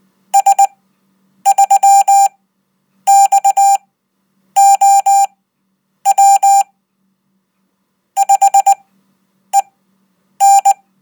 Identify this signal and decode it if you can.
Morse code, heard as a keyed tone, about 19 wpm, spaced as for 8 wpm, 790 Hz. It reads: S3XOW 5EN
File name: Code